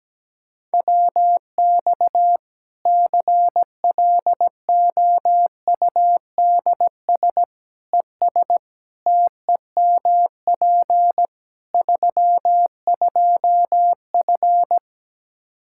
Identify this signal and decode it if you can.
Morse code, heard as a keyed tone, about 17 wpm, 700 Hz. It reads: WX CLOUDS ES TEMP 32F
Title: Practice QSO - Exchange 2 - Info - 17wpm